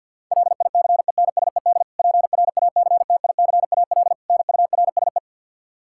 Morse code, heard as a keyed tone, about 52 wpm, 700 Hz. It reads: LICENSED PRACTICAL NURSE
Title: Practice QSO - Exchange 3 - More Info - 50wpm